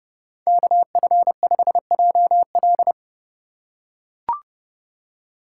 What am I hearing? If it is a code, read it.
Morse code, heard as a keyed tone, about 31 wpm, 700 Hz. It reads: KF5JL
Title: USA Call Signs - 5 Characters - 4x Character Spacing - 30wpm